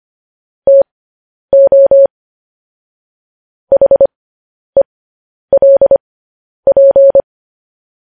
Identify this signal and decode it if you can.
Morse code, heard as a keyed tone, about 25 wpm, spaced as for 10 wpm, 550 Hz. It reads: TO HELP
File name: podcast / News headlines-1x-25x10